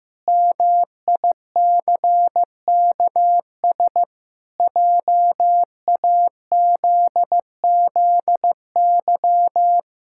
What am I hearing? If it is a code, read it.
Morse code, heard as a keyed tone, about 15 wpm, 700 Hz. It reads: MICKS JAZZY
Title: Pangram Challenge v2 - 15wpm